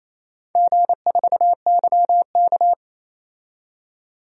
Morse code, heard as a keyed tone, about 28 wpm, 700 Hz. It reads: G4YK
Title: DXCC-100 Call Signs - 4 Characters - 28wpm